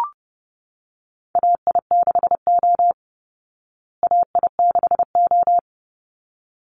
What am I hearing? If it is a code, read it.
Morse code, heard as a keyed tone, about 30 wpm, 700 Hz. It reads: AI6O AI6O
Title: Call Signs Repeated 3x - 30wpm